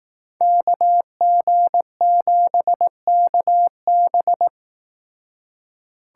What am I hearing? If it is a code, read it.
Morse code, heard as a keyed tone, about 19 wpm, 700 Hz. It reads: KG7KB